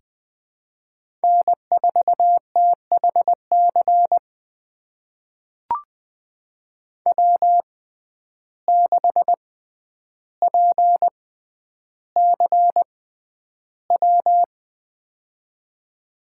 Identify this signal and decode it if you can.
Morse code, heard as a keyed tone, about 20 wpm, spaced as for 10 wpm, 700 Hz. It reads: N4THC W6PCW
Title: USA Call Signs - 5 Characters - 6x Character Spacing - 20wpm